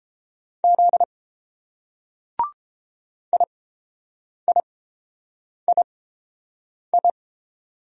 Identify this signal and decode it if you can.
Morse code, heard as a keyed tone, about 30 wpm, spaced as for 7 wpm, 700 Hz. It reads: Z IIII